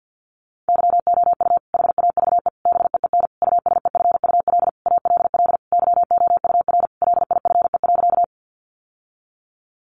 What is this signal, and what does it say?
Morse code, heard as a keyed tone, about 50 wpm, 700 Hz. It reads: YOU HAVE BEEN USEFUL ALL YOUR LIFE.
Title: Sentences from Top 700 Words 50wpm